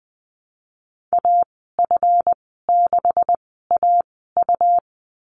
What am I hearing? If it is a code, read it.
Morse code, heard as a keyed tone, about 20 wpm, spaced as for 15 wpm, 700 Hz. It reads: AF6AU
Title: 186 Lesson 30g - Letter b - Call Signs 20wpm